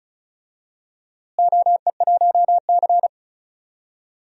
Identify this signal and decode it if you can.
Morse code, heard as a keyed tone, about 36 wpm, 700 Hz. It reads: OE1C
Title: DXCC-100 Call Signs - 4 Characters - Repeated 3x - 35wpm